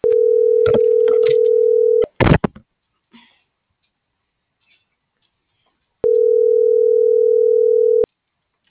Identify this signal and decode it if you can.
Ringback tone (North America), 440 + 480 Hz.